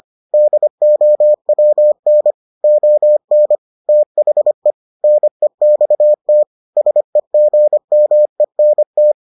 Morse code, heard as a keyed tone, about 25 wpm, 600 Hz. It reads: DOWN ON THE NEXT SEGMENT